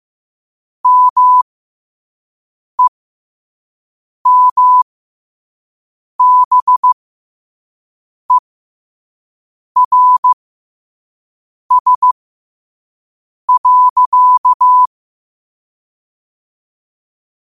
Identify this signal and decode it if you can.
Morse code, heard as a keyed tone, about 15 wpm, spaced as for 5 wpm, 1000 Hz. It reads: MEMBERS.
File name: qotdfiles / 5wpm